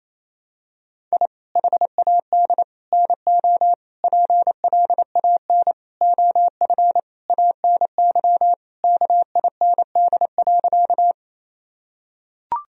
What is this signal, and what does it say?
Morse code, heard as a keyed tone, about 28 wpm, 700 Hz. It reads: I HAD NO PLAN OF ANY KIND.
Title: Sentences from Top 1000 Words - Review Part 2 - 28wpm